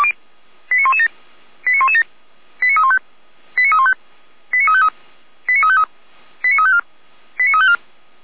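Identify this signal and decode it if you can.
Five-tone selective call (ZVEI), heard as quick runs of five tones, tones of about 70 ms. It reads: 89117 89117 89315 89315 89353 89353 89354 89354
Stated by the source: ZVEI five-tone selective call 89504 received on an SDR.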